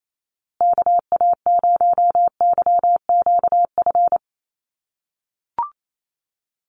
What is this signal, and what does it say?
Morse code, heard as a keyed tone, about 28 wpm, 700 Hz. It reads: KA0YQF
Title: USA Call Signs Repeated 2x 28wpm